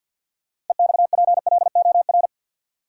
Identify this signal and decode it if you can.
Morse code, heard as a keyed tone, about 52 wpm, 700 Hz. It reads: EXPLOR
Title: Sentences from Top 1100 Words - New Word Drill Alt - 50wpm